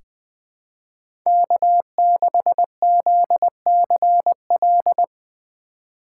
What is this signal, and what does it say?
Morse code, heard as a keyed tone, about 20 wpm, 700 Hz. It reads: K6ZCL
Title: K1USN SST QSO Practice - 20wpm